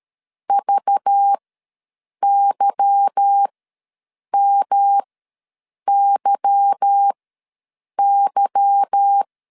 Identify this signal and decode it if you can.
Morse code, heard as a keyed tone, about 13 wpm, spaced as for 7 wpm, 800 Hz. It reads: VYMYY
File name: LEZIONI / 8 YJ45 / 8 Y ws 12wpm 3409